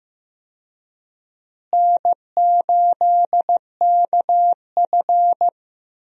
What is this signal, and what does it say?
Morse code, heard as a keyed tone, about 15 wpm, 700 Hz. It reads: N8KF
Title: USA Call Signs - 4 Characters - Repeated 2x - 15wpm